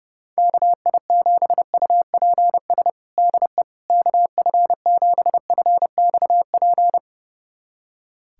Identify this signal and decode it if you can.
Morse code, heard as a keyed tone, about 30 wpm, 700 Hz. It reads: KI7UPS DE KF7FXP
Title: USA Callsign DE Callsign - 6 Characters - 30wpm